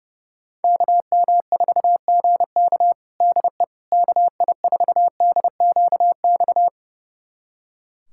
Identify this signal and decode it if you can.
Morse code, heard as a keyed tone, about 30 wpm, 700 Hz. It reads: KM4GK DE KI4DQX
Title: USA Callsign DE Callsign 30wpm